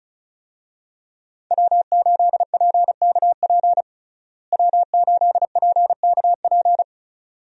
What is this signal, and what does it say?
Morse code, heard as a keyed tone, about 36 wpm, 700 Hz. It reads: W8PKP W8PKP
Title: USA Call Signs - 5 Characters - Repeated 2x - 35wpm